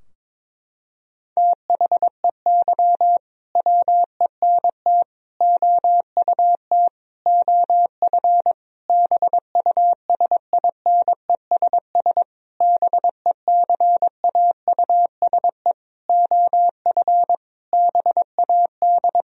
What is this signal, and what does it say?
Morse code, heard as a keyed tone, about 22 wpm, 700 Hz. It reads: THEY WENT OUT OF BUSINESS BECAUSE OF BAD
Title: Sentences from Top 1000 Words - Review Part 4 - 22wpm